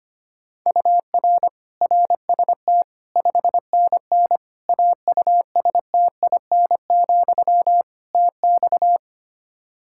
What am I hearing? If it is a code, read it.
Morse code, heard as a keyed tone, about 25 wpm, 700 Hz. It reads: UR RST 5NN AUSTIN, TX
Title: Practice QSO - Exchange 1 - Beginning - 25wpm